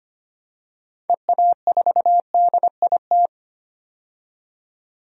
Morse code, heard as a keyed tone, about 25 wpm, 700 Hz. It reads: EA4DIT
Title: DXCC-100 Call Signs - 6 Characters - 25wpm